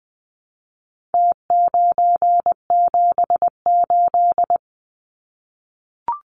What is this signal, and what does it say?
Morse code, heard as a keyed tone, about 20 wpm, 700 Hz. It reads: T978
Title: Numbers - Contest Style - 4-Digit 20wpm